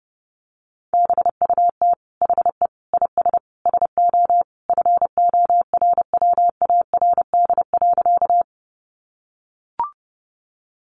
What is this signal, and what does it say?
Morse code, heard as a keyed tone, about 30 wpm, 700 Hz. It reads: BUT HE IS SO FORWARD.
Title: Sentences from Top 1300 Words 30wpm